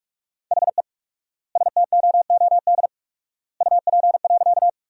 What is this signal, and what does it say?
Morse code, heard as a keyed tone, about 42 wpm, 700 Hz. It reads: HE STOOD UP.
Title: 4x Word-Spacing - Sentences from Top 500 Words - 45wpm